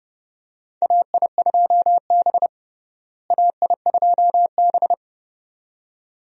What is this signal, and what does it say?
Morse code, heard as a keyed tone, about 32 wpm, 700 Hz. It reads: AI2B AI2B
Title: USA Call Signs - 4 Characters - Repeated 2x - 30wpm